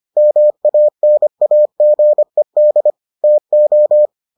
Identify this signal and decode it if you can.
Morse code, heard as a keyed tone, about 25 wpm, 600 Hz. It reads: MANAGED TO